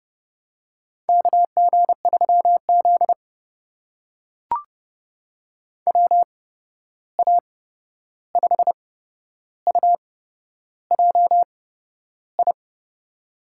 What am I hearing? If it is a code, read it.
Morse code, heard as a keyed tone, about 32 wpm, spaced as for 11 wpm, 700 Hz. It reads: KG3Z WA5UJI
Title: USA Call Signs - 8x Character Spacing - 30wpm